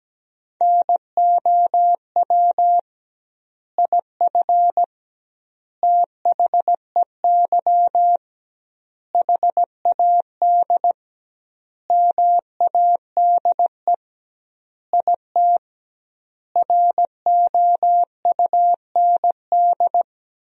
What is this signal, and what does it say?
Morse code, heard as a keyed tone, about 17 wpm, 700 Hz. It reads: NOW IF THEY HAD MADE IT ROUND
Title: 2x Word-Spacing - Sentences from Top 500 Words - 17wpm